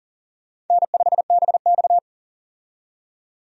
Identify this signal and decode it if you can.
Morse code, heard as a keyed tone, about 37 wpm, 700 Hz. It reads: N5BX
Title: USA Call Signs - 4 Characters - 40wpm